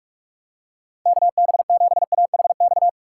Morse code, heard as a keyed tone, about 43 wpm, 700 Hz. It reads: KB7AHX
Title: USA Call Signs - 6 Characters - 6x Character Spacing - 45wpm